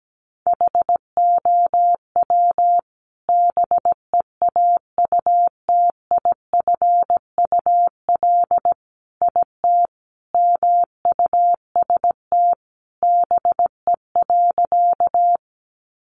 Morse code, heard as a keyed tone, about 17 wpm, 700 Hz. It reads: HOW BEAUTIFUL IT MUST BE.
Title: Sentences from Top 900 Words 17wpm